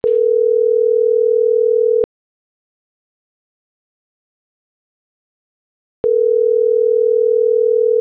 Ringback tone (North America), 440 + 480 Hz.